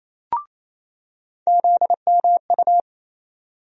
Morse code, heard as a keyed tone, about 28 wpm, 700 Hz. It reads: ZMU
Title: USA Call Sign Postfixes - 3 Letters 28wpm